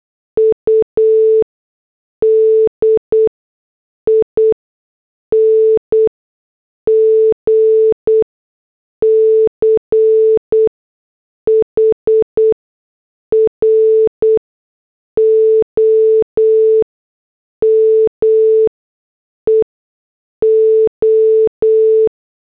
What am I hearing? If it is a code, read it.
Morse code, heard as a keyed tone, about 8 wpm, spaced as for 6 wpm, 440 Hz. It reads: UDING CHROME O
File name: morse / headlines